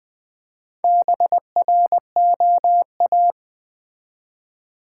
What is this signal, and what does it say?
Morse code, heard as a keyed tone, about 20 wpm, 700 Hz. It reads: BROA
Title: Build A Word - 1 Letter At A Time - 5-Letter Words - 20wpm